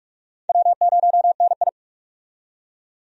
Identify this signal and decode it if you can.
Morse code, heard as a keyed tone, about 43 wpm, 700 Hz. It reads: W0NI
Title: USA Call Signs - 4 Characters - 45wpm